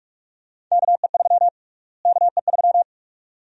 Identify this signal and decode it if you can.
Morse code, heard as a keyed tone, about 45 wpm, 700 Hz. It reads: KE3 KE3
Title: USA Call Sign Prefixes - Repeated 2x - 45wpm